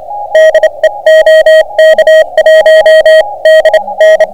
Morse code, heard as a keyed tone, about 30 wpm, spaced as for 25 wpm, 590 Hz, with noise. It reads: DE OK1DN